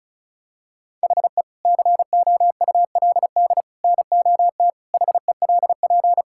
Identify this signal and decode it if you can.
Morse code, heard as a keyed tone, about 36 wpm, 700 Hz. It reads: HE COULD NOT HELP
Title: Sentences from Top 1600 Words 35wpm